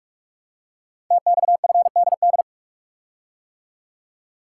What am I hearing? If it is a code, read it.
Morse code, heard as a keyed tone, about 44 wpm, 690 Hz. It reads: TXFDD
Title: Code Groups - 5 Letters - 45wpm